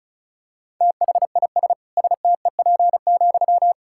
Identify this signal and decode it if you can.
Morse code, heard as a keyed tone, about 36 wpm, 700 Hz. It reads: THIS STEP,
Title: Sentences from Top 1000 Words - Review Part 6 - 35wpm